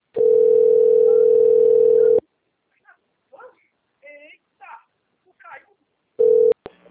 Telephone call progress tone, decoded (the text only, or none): ringback tone